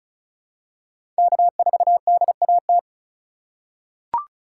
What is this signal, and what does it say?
Morse code, heard as a keyed tone, about 34 wpm, 700 Hz. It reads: K4DAT
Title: USA Call Signs - 5 Characters - 4x Character Spacing - 35wpm